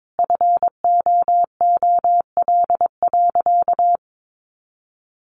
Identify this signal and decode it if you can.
Morse code, heard as a keyed tone, about 22 wpm, 700 Hz. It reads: FOOL.
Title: A Sound Of Thunder 22wpm